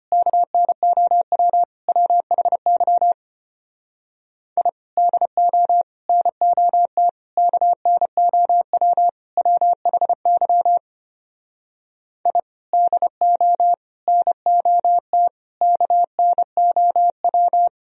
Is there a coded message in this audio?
Yes: Morse code from 0.1 to 17.7 s, heard as a keyed tone, about 33 wpm, 700 Hz. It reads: KNOW WHY I DO NOT KNOW WHY I DO NOT KNOW